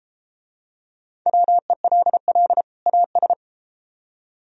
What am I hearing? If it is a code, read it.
Morse code, heard as a keyed tone, about 33 wpm, 700 Hz. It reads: WELL AS